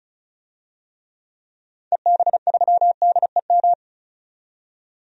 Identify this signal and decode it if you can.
Morse code, heard as a keyed tone, about 34 wpm, 700 Hz. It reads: EB3DEM
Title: DXCC-100 Call Signs - 6 Characters - 35wpm